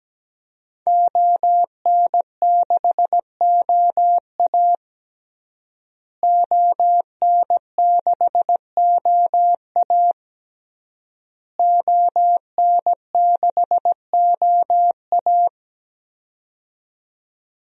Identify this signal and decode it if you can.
Morse code, heard as a keyed tone, about 17 wpm, 700 Hz. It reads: ON6OA ON6OA ON6OA
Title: DXCC-100 Call Signs - 5 Characters - Repeated 3x - 17wpm